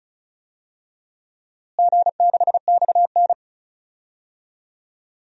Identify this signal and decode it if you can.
Morse code, heard as a keyed tone, about 36 wpm, 700 Hz. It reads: G6XN
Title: DXCC-100 Call Signs - 4 Characters - Repeated 2x - 35wpm